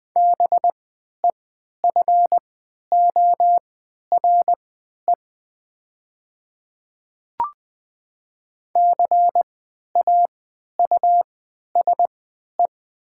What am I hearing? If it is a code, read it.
Morse code, heard as a keyed tone, about 20 wpm, spaced as for 11 wpm, 700 Hz. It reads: BEFORE CAUSE
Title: 3x Character Spacing - Top 200 Words - 20wpm